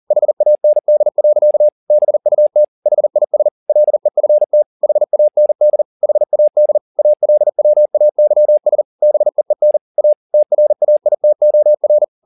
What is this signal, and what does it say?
Morse code, heard as a keyed tone, about 40 wpm, 600 Hz. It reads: HAND. BUT HIS LEFT HAND HAD ALWAYS BEEN A TRAITOR
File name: the old man and the sea 2 40wpm